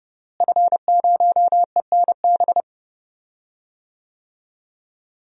Morse code, heard as a keyed tone, about 30 wpm, 700 Hz. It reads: F0ENB
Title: DXCC-100 Call Signs - 5 Characters - Repeated 3x - 30wpm